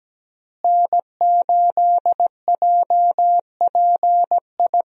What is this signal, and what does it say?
Morse code, heard as a keyed tone, about 17 wpm, 700 Hz. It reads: N8JPI
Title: USA Call Signs - 5 Characters - Repeated 2x - 17wpm